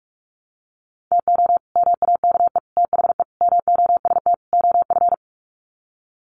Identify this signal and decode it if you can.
Morse code, heard as a keyed tone, about 44 wpm, 700 Hz. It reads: TO MAKE THE MOST OF
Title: Sets of 5 Words Encore2 45wpm